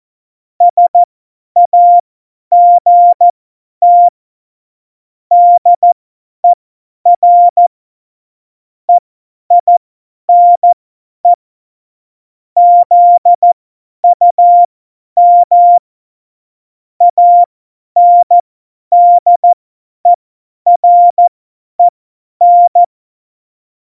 Morse code, heard as a keyed tone, about 14 wpm, spaced as for 10 wpm, 700 Hz. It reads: SAGT DER EINE ZUM ANDEREN
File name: Morsekurs / Witze / Witz 12. Juni 2018, 50 F70